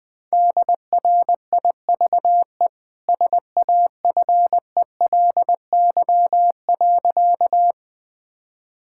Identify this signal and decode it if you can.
Morse code, heard as a keyed tone, about 21 wpm, 700 Hz. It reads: DRIVE SAFELY.
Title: Easy Sentences with Hints - 20wpm